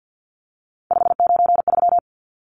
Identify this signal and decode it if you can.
Morse code, heard as a keyed tone, about 50 wpm, 700 Hz. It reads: 593
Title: Numbers - 3-Digit 50wpm